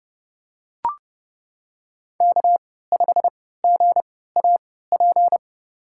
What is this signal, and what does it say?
Morse code, heard as a keyed tone, about 30 wpm, spaced as for 17 wpm, 700 Hz. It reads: K5GAP
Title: USA Call Signs - 5 Characters - 3x Character Spacing - 30wpm